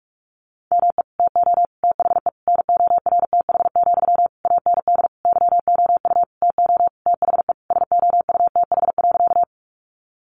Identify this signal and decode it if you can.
Morse code, heard as a keyed tone, about 46 wpm, 700 Hz. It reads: ME TO THE NORTH, AND YOU TO THE SOUTH.
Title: Sentences from Top 400 Words 45wpm